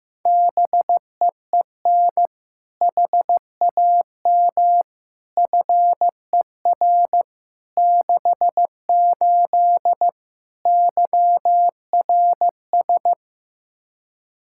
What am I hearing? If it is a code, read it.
Morse code, heard as a keyed tone, about 15 wpm, 700 Hz. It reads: BEEN HAM FER 68 YRS
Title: Practice QSO - Exchange 3 - More Info - 15wpm